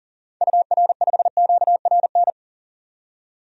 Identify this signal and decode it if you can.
Morse code, heard as a keyed tone, about 40 wpm, 700 Hz. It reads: UR5QRN
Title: DXCC-100 Call Signs - 6 Characters - 40wpm